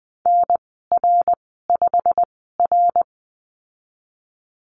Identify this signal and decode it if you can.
Morse code, heard as a keyed tone, about 20 wpm, spaced as for 15 wpm, 700 Hz. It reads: NR5R
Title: 128 Lesson 21g - Number 6 - Call Signs 20wpm